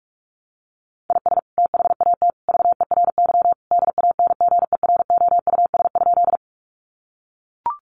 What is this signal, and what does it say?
Morse code, heard as a keyed tone, about 44 wpm, 700 Hz. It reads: IS THAT VERY DANGEROUS?
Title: Sentences from Top 1200 Words 45wpm